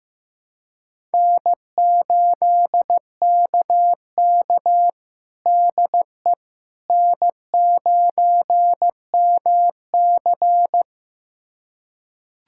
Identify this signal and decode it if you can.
Morse code, heard as a keyed tone, about 16 wpm, 700 Hz. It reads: N8KK DE N9MC
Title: USA Callsign DE Callsign - 4 Characters - 15wpm